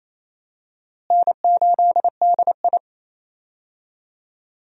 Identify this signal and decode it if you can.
Morse code, heard as a keyed tone, about 28 wpm, 700 Hz. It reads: N8DI
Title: USA Call Signs - 4 Characters - Repeated 3x - 28wpm